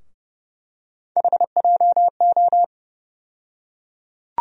Morse code, heard as a keyed tone, about 31 wpm, 700 Hz. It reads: HJO
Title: USA Call Sign Postfixes - 3 Letters 30wpm